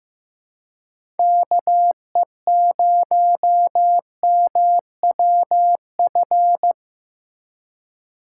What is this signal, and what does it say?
Morse code, heard as a keyed tone, about 15 wpm, 700 Hz. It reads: KE0MWF
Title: USA Call Signs - 6 Characters - 3x Character Spacing - 15wpm